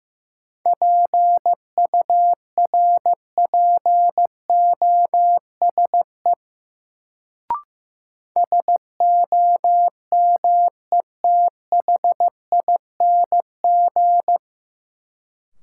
Morse code, heard as a keyed tone, about 15 wpm, 700 Hz. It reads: PURPOSE SOMETHING
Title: Sets of 5 Words - New Word Drill - 15wpm